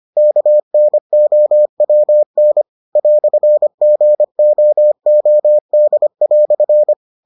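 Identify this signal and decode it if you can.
Morse code, heard as a keyed tone, about 25 wpm, 600 Hz. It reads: KNOWN "GOOD"
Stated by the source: Morse edition of the Foundations of Amateur Radio episodes.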